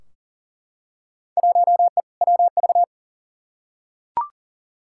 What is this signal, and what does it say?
Morse code, heard as a keyed tone, about 40 wpm, 700 Hz. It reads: 1E WV
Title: Field Day Exchanges Encore - 40wpm